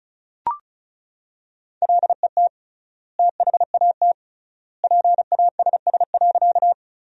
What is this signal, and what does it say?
Morse code, heard as a keyed tone, about 34 wpm, 700 Hz. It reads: LET THAT PASS.
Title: 3x Word-Spacing - Sentences from Top 400 Words - 35wpm